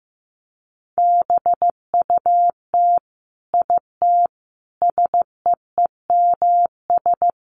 Morse code, heard as a keyed tone, about 14 wpm, 700 Hz. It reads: BUT IT SEEMS
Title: Sets of 5 Words Encore2 15wpm